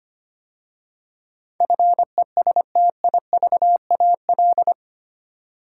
Morse code, heard as a keyed tone, about 25 wpm, 700 Hz. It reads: FESTIVAL
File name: 8 Letter Words - First 3 Letters Given - 25wpm